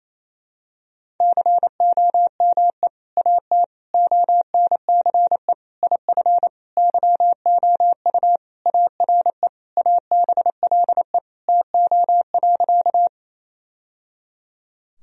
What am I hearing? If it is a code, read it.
Morse code, heard as a keyed tone, about 28 wpm, 700 Hz. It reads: COME AT ONCE IF YOU ARE ABLE TO.
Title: Sentences from Top 1000 Words - Review Part 2 - 28wpm